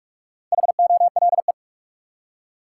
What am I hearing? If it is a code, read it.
Morse code, heard as a keyed tone, about 42 wpm, 700 Hz. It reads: HOLE
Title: Four Letter Words - 45wpm